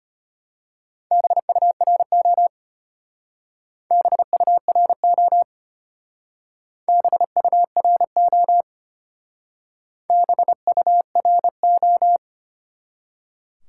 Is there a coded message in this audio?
Yes: Morse code from 1.1 to 12.2 s, heard as a keyed tone, about 32 wpm, 700 Hz. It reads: BURO BURO BURO BURO